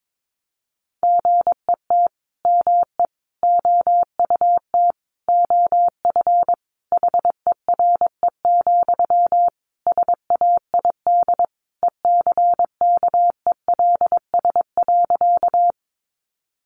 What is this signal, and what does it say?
Morse code, heard as a keyed tone, about 22 wpm, 700 Hz. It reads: GET ME OUT OF HERE, SAID ECKELS.
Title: A Sound Of Thunder 22wpm